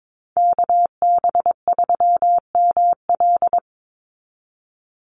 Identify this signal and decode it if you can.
Morse code, heard as a keyed tone, about 22 wpm, 700 Hz. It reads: KB3ML
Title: Call Signs Repeated 3x - 22wpm